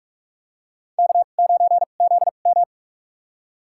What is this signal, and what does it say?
Morse code, heard as a keyed tone, about 43 wpm, spaced as for 32 wpm, 700 Hz. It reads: K9ZM
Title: USA Call Signs - 4 Characters - 2x Character Spacing - 45wpm